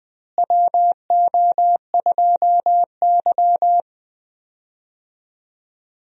Morse code, heard as a keyed tone, about 20 wpm, 700 Hz. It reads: WO2Y
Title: USA Call Signs - 4 Characters - Repeated 3x - 20wpm